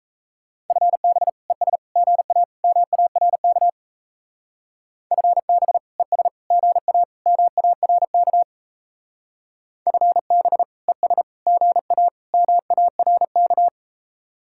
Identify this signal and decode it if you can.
Morse code, heard as a keyed tone, about 33 wpm, 700 Hz. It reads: FB ES GA MARK FB ES GA MARK FB ES GA MARK
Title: Speed-Racing Common QSO Phrases Encore - 28wpm